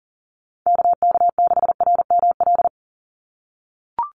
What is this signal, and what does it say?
Morse code, heard as a keyed tone, about 40 wpm, 700 Hz. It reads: KK6RML
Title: USA Call Signs 40wpm